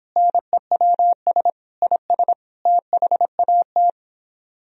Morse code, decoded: NEWS IS THAT

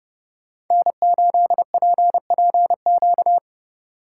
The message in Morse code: N8PPQ

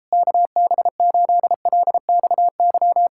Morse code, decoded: KB8LXY

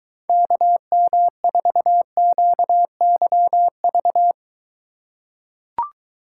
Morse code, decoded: KM4QYV

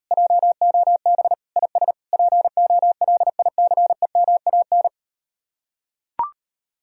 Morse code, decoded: JOB IS POLICEMAN

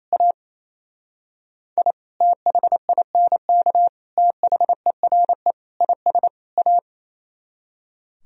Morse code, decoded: A I THINK THERE IS A

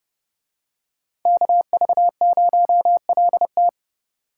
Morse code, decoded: KV0LT

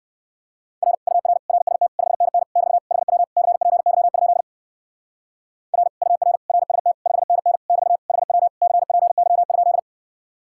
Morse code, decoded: R UR RST 5NN <BT> HW CPY? R UR RST 5NN <BT> HW CPY?

